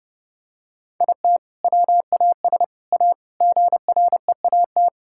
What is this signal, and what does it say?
Morse code, heard as a keyed tone, about 30 wpm, 700 Hz. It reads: IT WAS A GREAT
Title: Sentences from Top 1000 Words - Review Part 6 - 30wpm